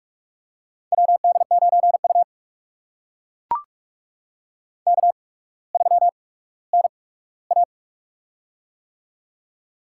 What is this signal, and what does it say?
Morse code, heard as a keyed tone, about 44 wpm, spaced as for 19 wpm, 700 Hz. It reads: WD9U K3NA
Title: USA Call Signs - 4 Characters - 8x Character Spacing - 45wpm